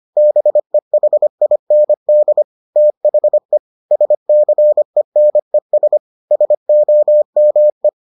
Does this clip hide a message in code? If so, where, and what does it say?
Yes: Morse code from 0.2 to 7.9 s, heard as a keyed tone, about 25 wpm, 600 Hz. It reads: BEHIND THE SCENES SOME